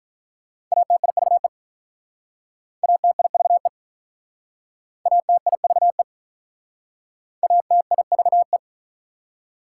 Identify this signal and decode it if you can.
Morse code, heard as a keyed tone, about 47 wpm, 690 Hz. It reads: ATIVE ATIVE ATIVE ATIVE